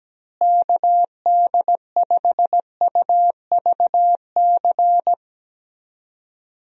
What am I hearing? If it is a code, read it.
Morse code, heard as a keyed tone, about 17 wpm, 700 Hz. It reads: KD5UVC
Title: USA Call Signs - 6 Characters - Repeated 3x - 17wpm